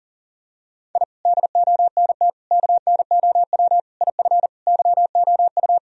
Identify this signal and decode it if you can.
Morse code, heard as a keyed tone, about 40 wpm, 700 Hz. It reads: I DONT KNOW IF YOU
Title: Sentences from Top 1000 Words - Review Part 2 - 40wpm